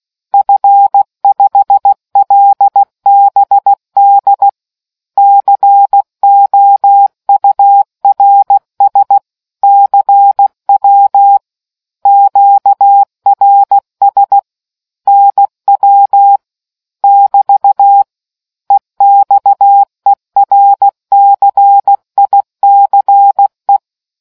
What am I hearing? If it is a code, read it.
Morse code, heard as a keyed tone, about 16 wpm, 800 Hz. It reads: F5LBD COURS CW QRS NW <BT> EXERCICE